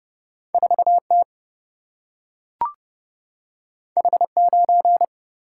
Morse code, heard as a keyed tone, about 31 wpm, 700 Hz. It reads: 4T H9
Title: Single-Letter Single-Number 30wpm